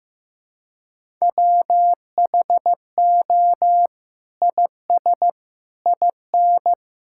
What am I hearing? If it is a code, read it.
Morse code, heard as a keyed tone, about 15 wpm, 700 Hz. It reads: WHO IS IN
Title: Sentences from Top 1000 Words - Review Part 6 - 15wpm